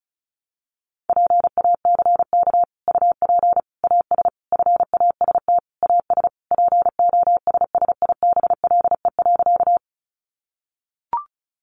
Morse code, heard as a keyed tone, about 34 wpm, 700 Hz. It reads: PACK UP AS FAST AS POSSIBLE.
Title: Sentences from Top 1300 Words 35wpm